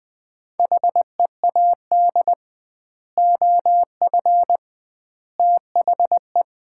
Morse code, heard as a keyed tone, about 20 wpm, 700 Hz. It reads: HEAD OF THE
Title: 2x Word-Spacing - Sets of 3 Words - 20wpm